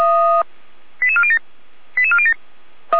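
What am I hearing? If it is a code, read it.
Five-tone selective call (ZVEI), heard as quick runs of five tones, tones of about 70 ms. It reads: 88407 88407
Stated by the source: Received by SDR, ZVEI five-tone.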